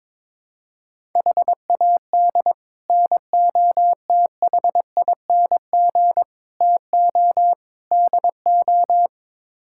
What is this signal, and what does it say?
Morse code, heard as a keyed tone, about 22 wpm, 700 Hz. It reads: HAD NOTHING TO DO
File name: practice / 22wpm / Build A Phrase - 5-Word Phrases - 22wpm